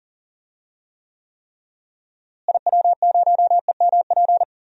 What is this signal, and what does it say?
Morse code, heard as a keyed tone, about 40 wpm, 700 Hz. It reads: IW0EMP
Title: DXCC-100 Call Signs - 6 Characters - Repeated 3x - 40wpm